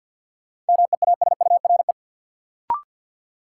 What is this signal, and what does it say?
Morse code, heard as a keyed tone, about 53 wpm, 700 Hz. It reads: MEASURE
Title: Sentences from Top 1200 Words - New Word Drill Alt - 50wpm